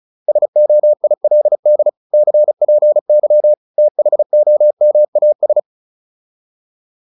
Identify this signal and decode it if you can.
Morse code, heard as a keyed tone, about 34 wpm, 600 Hz. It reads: SOILD CPY THOMAS